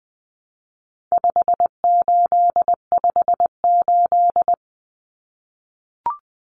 Morse code, heard as a keyed tone, about 20 wpm, 700 Hz. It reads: 5858
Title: Numbers - 4-Digit 20wpm